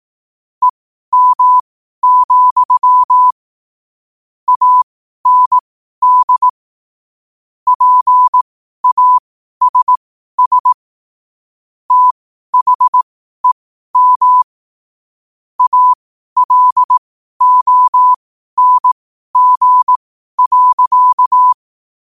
Morse code, heard as a keyed tone, about 18 wpm, spaced as for 12 wpm, 1000 Hz. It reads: EM, AND PASS THEM ALONG.
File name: qotdfiles / 12wpm